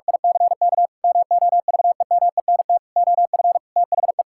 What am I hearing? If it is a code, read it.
Morse code, heard as a keyed tone, about 46 wpm, 700 Hz. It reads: ICK MOVEMENT OF THE
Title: Pangram Challenge v2 - 45wpm